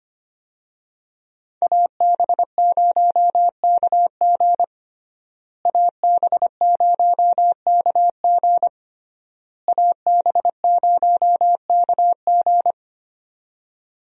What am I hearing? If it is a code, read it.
Morse code, heard as a keyed tone, about 25 wpm, 700 Hz. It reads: AB0KG AB0KG AB0KG
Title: USA Call Signs - 5 Characters - Repeated 3x - 25wpm